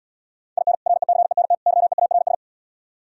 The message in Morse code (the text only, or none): IN RECENT YEARS